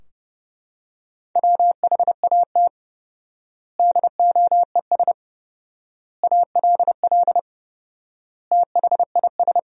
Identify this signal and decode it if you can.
Morse code, heard as a keyed tone, about 32 wpm, 700 Hz. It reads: WHAT DOES ALL THIS
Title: 4x Word-Spacing - Sentences from Top 200 Words - 30wpm